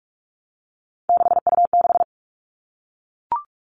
Morse code, heard as a keyed tone, about 44 wpm, 700 Hz. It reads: 6U6
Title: All Cut Numbers - 3-Digit 45wpm